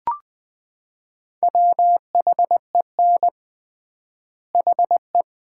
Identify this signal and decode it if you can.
Morse code, heard as a keyed tone, about 20 wpm, 700 Hz. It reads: WHEN HE
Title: 3x Word-Spacing - Sets of 3 Words - 20wpm